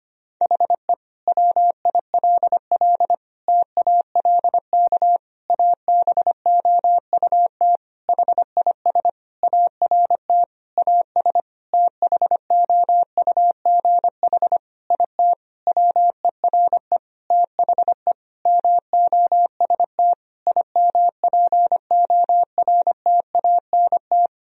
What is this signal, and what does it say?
Morse code, heard as a keyed tone, about 25 wpm, 700 Hz. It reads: HE WILL TALK ABOUT HIS ART AS THOUGH IT WERE THE MOST IMPORTANT
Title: Sentences from Top 1000 Words - Review Part 3 - 25wpm